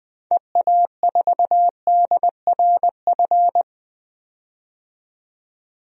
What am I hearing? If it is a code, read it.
Morse code, heard as a keyed tone, about 20 wpm, 700 Hz. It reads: EA4DRF
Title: DXCC-100 Call Signs - 6 Characters - 20wpm